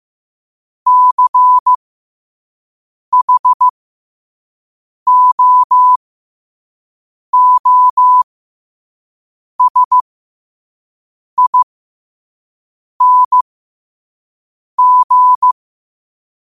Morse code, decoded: CHOOSING

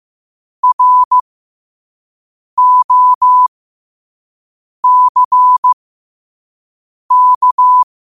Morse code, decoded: ROCK

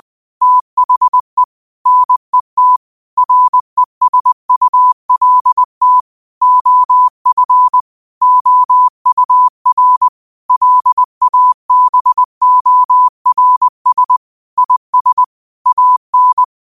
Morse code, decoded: THE NET RESULT OF OUR LABORS IS AN

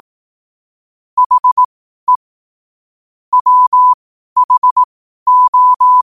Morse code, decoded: HE WHO

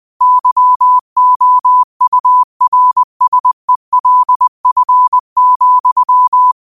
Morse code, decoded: YOURSELF,

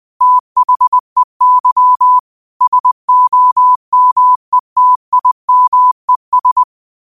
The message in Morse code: THEY SOMETIMES

